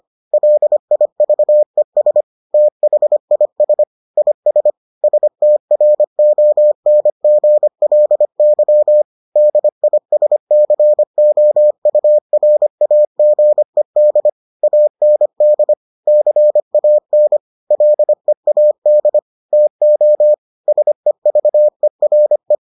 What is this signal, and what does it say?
Morse code, heard as a keyed tone, about 25 wpm, 600 Hz. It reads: LIVES THIS IS STRONGLY DISCOURAGED AND CAN LEAD TO SEVERE